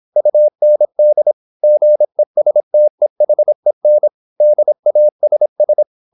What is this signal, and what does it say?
Morse code, heard as a keyed tone, about 26 wpm, 600 Hz. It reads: UND GESTEHEN DASS